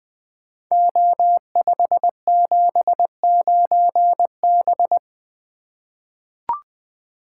Morse code, heard as a keyed tone, about 20 wpm, 700 Hz. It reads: O579B
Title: Code Groups - 5 Letter:Numbers - 20wpm